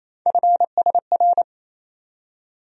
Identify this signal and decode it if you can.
Morse code, heard as a keyed tone, about 27 wpm, 700 Hz. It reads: FSR
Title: USA Call Sign Postfixes - 3 Letters 28wpm